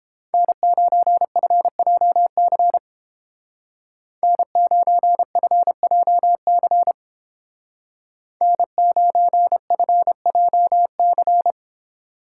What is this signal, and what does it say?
Morse code, heard as a keyed tone, about 29 wpm, 700 Hz. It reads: N9FJC N9FJC N9FJC